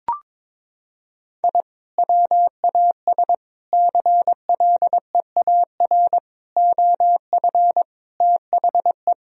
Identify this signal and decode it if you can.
Morse code, heard as a keyed tone, about 22 wpm, 700 Hz. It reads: I WAS CLEAR OF THE
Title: Sentences from Top 1000 Words - Review Part 2 - 22wpm